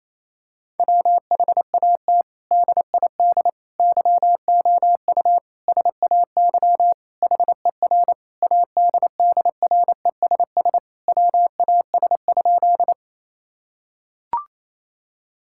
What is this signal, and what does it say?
Morse code, heard as a keyed tone, about 28 wpm, 700 Hz. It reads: WHAT DID YOU SAY HER ADDRESS WAS?
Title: Sentences from Top 1000 Words - Review Part 6 - 28wpm